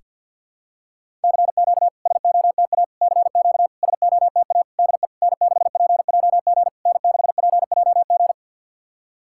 Morse code, heard as a keyed tone, about 51 wpm, 690 Hz. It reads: CQ SOTA CQ SOTA DE N6PJG N6PJG
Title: SOTA QSO Practice 50wpm